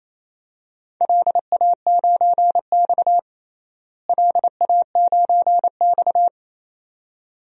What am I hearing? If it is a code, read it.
Morse code, heard as a keyed tone, about 28 wpm, 700 Hz. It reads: LA9X LA9X
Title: DXCC-100 Call Signs - 4 Characters - Repeated 2x - 28wpm